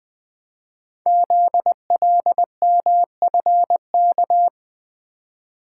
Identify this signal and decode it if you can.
Morse code, heard as a keyed tone, about 20 wpm, 700 Hz. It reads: ZLMFK
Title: Code Groups - 5 Letters - 20wpm